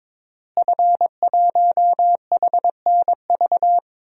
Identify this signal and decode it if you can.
Morse code, heard as a keyed tone, about 22 wpm, 700 Hz. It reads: F1HNV
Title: DXCC-100 Call Signs - 5 Characters - Repeated 2x - 22wpm